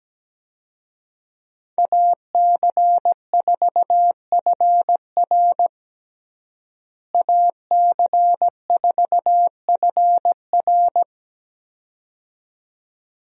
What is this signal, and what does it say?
Morse code, heard as a keyed tone, about 17 wpm, 700 Hz. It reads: AC4FR AC4FR
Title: USA Call Signs - 5 Characters - Repeated 2x - 17wpm